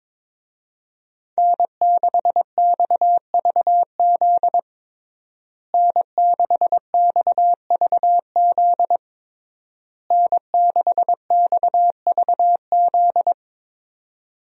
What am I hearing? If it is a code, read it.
Morse code, heard as a keyed tone, about 22 wpm, 700 Hz. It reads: N6XVZ N6XVZ N6XVZ
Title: USA Call Signs - 5 Characters - Repeated 3x - 22wpm